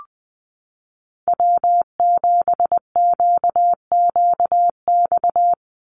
Morse code, heard as a keyed tone, about 20 wpm, 700 Hz. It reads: W7QQX
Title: USA Call Signs 20wpm